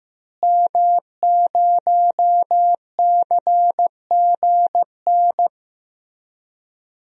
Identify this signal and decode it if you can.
Morse code, heard as a keyed tone, about 15 wpm, 700 Hz. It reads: M0CGN
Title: DXCC-100 Call Signs - 5 Characters - Repeated 3x - 15wpm